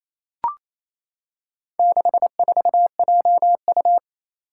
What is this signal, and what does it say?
Morse code, heard as a keyed tone, about 28 wpm, 700 Hz. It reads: 64JU
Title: Four Challenging Letter-Number Combinations 28wpm